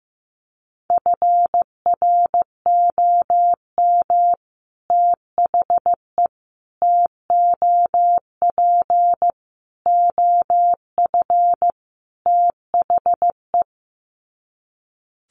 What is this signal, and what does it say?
Morse code, heard as a keyed tone, about 15 wpm, 700 Hz. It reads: FROM THE TOP OF THE
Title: Sets of 5 Words Encore 15wpm